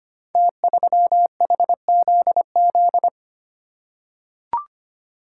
Morse code, heard as a keyed tone, about 25 wpm, 700 Hz. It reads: T3HZZ
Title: Code Groups - 5 Letter:Numbers - 25wpm